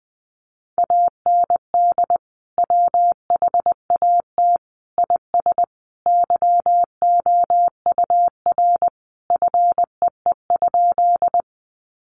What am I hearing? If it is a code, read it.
Morse code, heard as a keyed tone, about 19 wpm, 700 Hz. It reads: AND WHAT IS YOUR FEE?
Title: Sentences from Top 1500 Words 20wpm